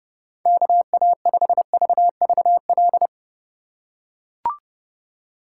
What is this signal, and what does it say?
Morse code, heard as a keyed tone, about 32 wpm, 700 Hz. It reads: KA5VVL